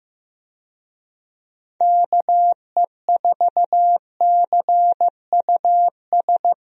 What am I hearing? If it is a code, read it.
Morse code, heard as a keyed tone, about 15 wpm, 700 Hz. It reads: KE4CUS
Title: USA Call Signs - 6 Characters - 15wpm